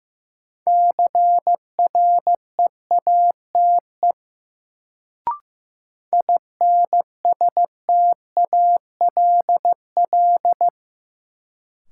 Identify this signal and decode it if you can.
Morse code, heard as a keyed tone, about 15 wpm, 700 Hz. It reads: CREATE INSTALL
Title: Sentences from Top 1600 Words - New Word Drill Alt - 15wpm